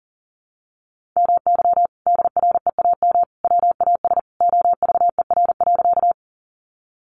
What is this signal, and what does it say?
Morse code, heard as a keyed tone, about 40 wpm, 700 Hz. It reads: MY DREAM WAS OVER.
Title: Sentences from Top 1500 Words 40wpm